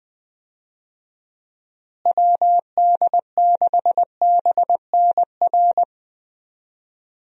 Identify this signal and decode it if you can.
Morse code, heard as a keyed tone, about 20 wpm, 700 Hz. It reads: WD6BNR
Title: USA Call Signs - 6 Characters - Repeated 3x - 20wpm